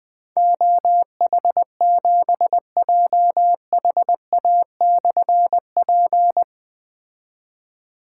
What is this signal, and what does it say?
Morse code, heard as a keyed tone, about 20 wpm, 700 Hz. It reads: OH7JHA/P
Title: Difficult Call Signs 20wpm